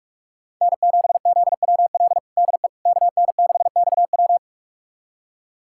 Morse code, heard as a keyed tone, about 46 wpm, 700 Hz. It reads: N7ZWL DE KN6XW
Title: USA Callsign DE Callsign - 5 Characters - 45wpm